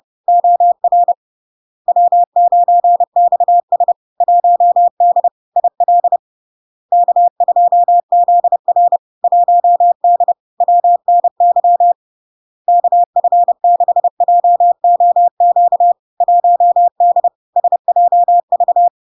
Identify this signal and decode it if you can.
Morse code, heard as a keyed tone, about 30 wpm, 700 Hz. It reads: OR W9XS 1D IL K2ZR 1D WNY KF6JOQ 1D SJV
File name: LARK-FD-2022-Log-in-CW-30wpm-700-Hz